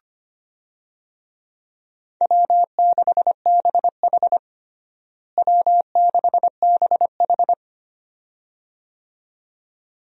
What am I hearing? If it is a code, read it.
Morse code, heard as a keyed tone, about 25 wpm, 700 Hz. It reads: W6BH W6BH
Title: USA Call Signs - 4 Characters - Repeated 2x - 25wpm